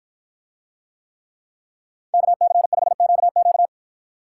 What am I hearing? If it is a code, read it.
Morse code, heard as a keyed tone, about 53 wpm, 700 Hz. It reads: KC5QQ